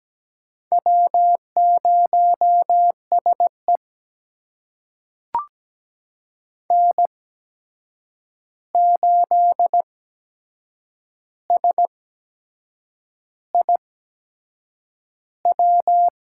Morse code, heard as a keyed tone, about 17 wpm, spaced as for 7 wpm, 700 Hz. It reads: W0SE N8SIW
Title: USA Call Signs - 8x Character Spacing - 17wpm